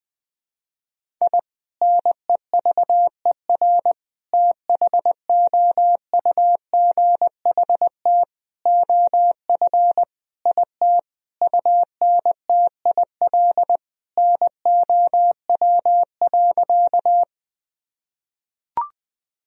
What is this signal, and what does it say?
Morse code, heard as a keyed tone, about 20 wpm, 700 Hz. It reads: I NEVER THOUGHT OF IT UNTIL NOW.
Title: Sentences from Top 1000 Words - Review Part 1 - 20wpm